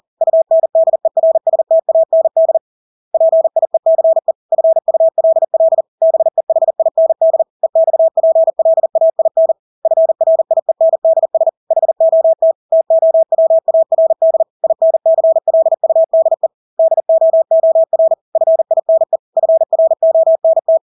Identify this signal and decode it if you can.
Morse code, heard as a keyed tone, about 40 wpm, 650 Hz. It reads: UNDERSTAND PIECE FULL BEHIND EXPLAIN FRIENDS HOT TOWARD INCLUDE DOOR FINE FRONT
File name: Engelske ord 40-40 wpm